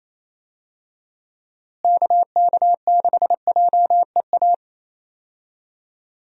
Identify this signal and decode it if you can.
Morse code, heard as a keyed tone, about 28 wpm, 700 Hz. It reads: KK6JEA
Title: USA Call Signs - 6 Characters - Repeated 2x - 28wpm